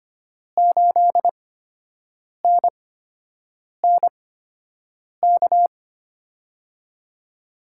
Morse code, heard as a keyed tone, about 25 wpm, spaced as for 7 wpm, 700 Hz. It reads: 8NNK
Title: USA Call Signs - 5 Characters - 8x Character Spacing - 25wpm